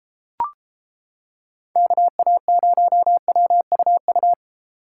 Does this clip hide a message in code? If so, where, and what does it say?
Yes: Morse code from 1.8 to 4.3 s, heard as a keyed tone, about 28 wpm, 700 Hz. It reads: KA0WUU